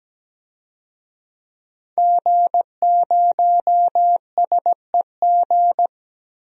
Morse code, heard as a keyed tone, about 17 wpm, 700 Hz. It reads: G0SEG
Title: DXCC-100 Call Signs - 5 Characters - Repeated 3x - 17wpm